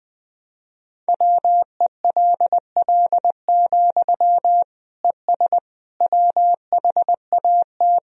Morse code, heard as a keyed tone, about 20 wpm, 700 Hz. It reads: WELL, ES WHAT
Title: Sentences from Top 200 Words and Abbreviations 20wpm